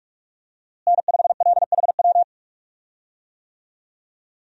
Morse code, heard as a keyed tone, about 46 wpm, 700 Hz. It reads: N5LHW
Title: USA Call Signs - 5 Characters - Repeated 3x - 45wpm